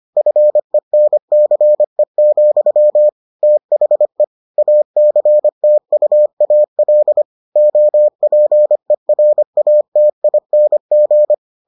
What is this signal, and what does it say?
Morse code, heard as a keyed tone, about 25 wpm, 600 Hz. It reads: FENCE, THE ACTUAL OPERATING